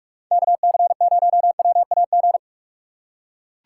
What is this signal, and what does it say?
Morse code, heard as a keyed tone, about 44 wpm, 700 Hz. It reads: KC0WAG
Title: USA Call Signs - 6 Characters - 6x Character Spacing - 45wpm